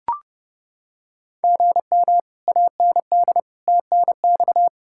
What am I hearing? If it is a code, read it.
Morse code, heard as a keyed tone, about 30 wpm, 700 Hz. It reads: GM AND TNX